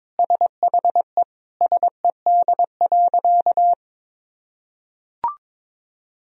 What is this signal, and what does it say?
Morse code, heard as a keyed tone, about 21 wpm, 700 Hz. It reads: SHE SED.
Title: Sentences from Top 100 Words and Abbreviations 22wpm